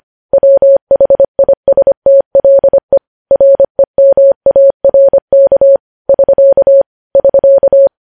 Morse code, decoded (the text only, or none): WHISTLE REMARK <SK> <SK>